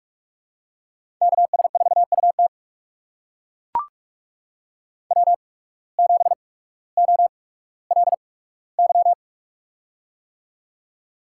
Morse code, heard as a keyed tone, about 44 wpm, spaced as for 19 wpm, 700 Hz. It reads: KS4UT W7OLY
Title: USA Call Signs - 8x Character Spacing - 45wpm